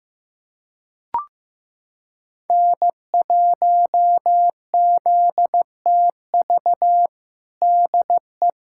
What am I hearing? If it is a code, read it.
Morse code, heard as a keyed tone, about 15 wpm, 700 Hz. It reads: N1ZTV DE
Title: USA Callsign DE Callsign 15wpm